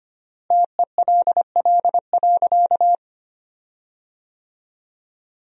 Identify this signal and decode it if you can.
Morse code, heard as a keyed tone, about 25 wpm, 700 Hz. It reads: TELL.
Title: 3x Word-Spacing - Sentences from Top 200 Words - 25wpm